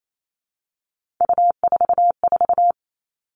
Morse code, heard as a keyed tone, about 28 wpm, 700 Hz. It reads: U44
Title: All Cut Numbers - 3-Digit 28wpm